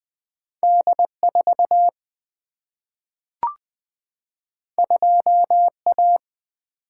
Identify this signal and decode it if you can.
Morse code, heard as a keyed tone, about 20 wpm, 700 Hz. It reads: D4 2A
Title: Single-Letter Single-Number 20wpm